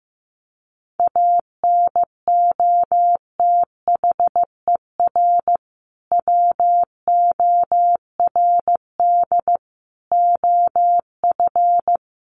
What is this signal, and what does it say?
Morse code, heard as a keyed tone, about 15 wpm, 700 Hz. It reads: ANOTHER WORD OF
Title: Sentences from Top 1100 Words 15wpm